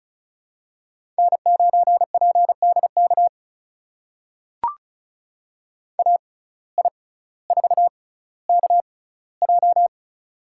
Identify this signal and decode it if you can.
Morse code, heard as a keyed tone, about 36 wpm, spaced as for 18 wpm, 700 Hz. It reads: N9PDK AI4KJ
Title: USA Call Signs - 5 Characters - 6x Character Spacing - 35wpm